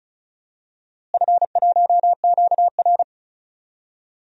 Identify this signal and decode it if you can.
Morse code, heard as a keyed tone, about 36 wpm, 700 Hz. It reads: F1QR
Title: DXCC-100 Call Signs - 4 Characters - Repeated 2x - 35wpm